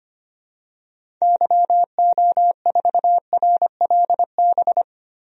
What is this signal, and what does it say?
Morse code, heard as a keyed tone, about 25 wpm, 700 Hz. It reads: YO4RLB
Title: DXCC-100 Call Signs - 6 Characters - Repeated 2x - 25wpm